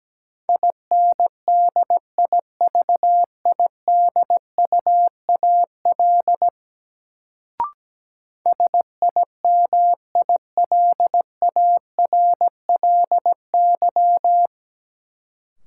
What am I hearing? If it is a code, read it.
Morse code, heard as a keyed tone, about 17 wpm, 700 Hz. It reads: INDIVIDUAL SIMILARLY
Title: Sentences from Top 1400 Words - New Word Drill Alt - 17wpm